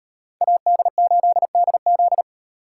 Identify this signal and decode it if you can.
Morse code, heard as a keyed tone, about 38 wpm, 700 Hz. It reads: AD8DZ